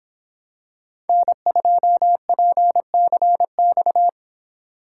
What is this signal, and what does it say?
Morse code, heard as a keyed tone, about 26 wpm, 700 Hz. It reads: N2PCX